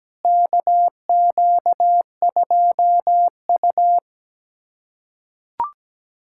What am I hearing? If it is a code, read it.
Morse code, heard as a keyed tone, about 17 wpm, 700 Hz. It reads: KQ2U
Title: USA Callsign DE Callsign - 4 Characters - 17wpm